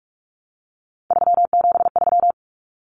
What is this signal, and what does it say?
Morse code, heard as a keyed tone, about 44 wpm, 700 Hz. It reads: 373
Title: Numbers - 3-Digit 45wpm